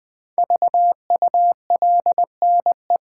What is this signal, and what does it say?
Morse code, heard as a keyed tone, about 20 wpm, 700 Hz. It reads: VULNE
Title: Build A Word - 5 Letters At A Time - 10-Letter Words - 20wpm